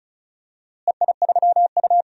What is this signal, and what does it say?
Morse code, heard as a keyed tone, about 34 wpm, 700 Hz. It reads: EI3U
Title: DXCC-100 Call Signs - 4 Characters - Repeated 3x - 35wpm